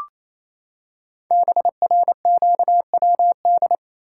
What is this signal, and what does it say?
Morse code, heard as a keyed tone, about 28 wpm, 700 Hz. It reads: BRQWD
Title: Code Groups - 5 Letters - 28wpm